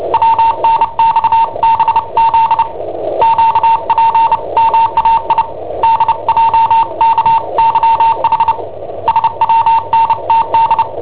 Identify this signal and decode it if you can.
Morse code, heard as a keyed tone, about 29 wpm, 910 Hz. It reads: WNXBZ QPMAI DJKYH SWNTD
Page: Cassetta esercizi preparata da I7OHP per IW5ELP